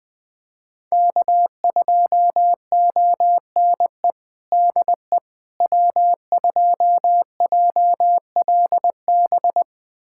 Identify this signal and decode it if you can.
Morse code, heard as a keyed tone, about 21 wpm, 700 Hz. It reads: K2ONE DE W2JLB
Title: USA Callsign DE Callsign - 5 Characters - 20wpm